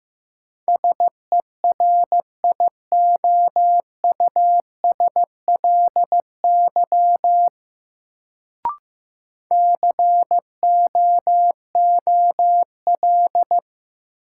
Morse code, heard as a keyed tone, about 15 wpm, 700 Hz. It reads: SERIOUSLY COOL
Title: Sentences from Top 1200 Words - New Word Drill - 15wpm